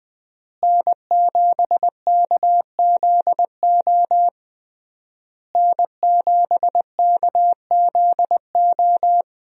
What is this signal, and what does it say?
Morse code, heard as a keyed tone, about 20 wpm, 700 Hz. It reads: N7KZO N7KZO
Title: USA Call Signs - 5 Characters - Repeated 3x - 20wpm